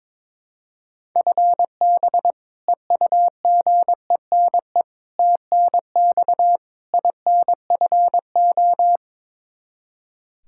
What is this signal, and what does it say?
Morse code, heard as a keyed tone, about 22 wpm, 700 Hz. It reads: FB EUGENE TNX INFO
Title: Practice QSO - Exchange 3 - More Info - 22wpm